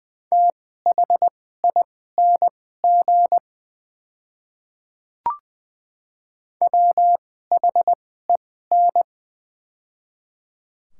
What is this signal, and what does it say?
Morse code, heard as a keyed tone, about 20 wpm, spaced as for 14 wpm, 700 Hz. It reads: THING WHEN
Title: 2x Character Spacing - Top 200 Words - 20wpm